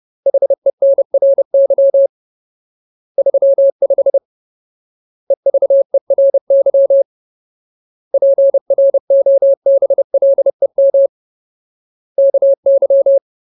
Morse code, heard as a keyed tone, about 30 wpm, 550 Hz. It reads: HENRY 35 EVERY PROBLEM KY
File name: Ultimate Word List / 30WPM Ultimate